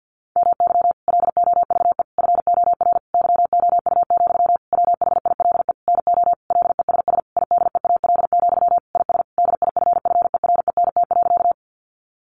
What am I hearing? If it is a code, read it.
Morse code, heard as a keyed tone, about 50 wpm, 700 Hz. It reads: MY LOVE FOR YOU, WHILE NO LESS IDEAL, IS DIFFERENT.
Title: Sentences from Top 1400 Words 50wpm